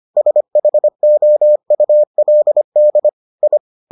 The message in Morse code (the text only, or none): SHOULD I